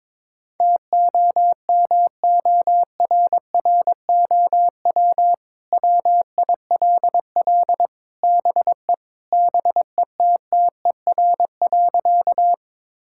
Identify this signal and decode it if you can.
Morse code, heard as a keyed tone, about 22 wpm, 700 Hz. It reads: TOMORROW WILL BE BETTER.
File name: practice / 22wpm / Sentences with 2-Word Phrases - 22wpm